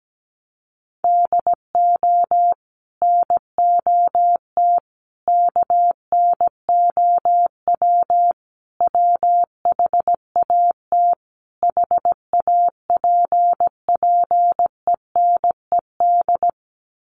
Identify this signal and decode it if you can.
Morse code, heard as a keyed tone, about 17 wpm, 700 Hz. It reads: DO NOT KNOW WHAT HAPPENED
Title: Sets of 5 Words Encore2 17wpm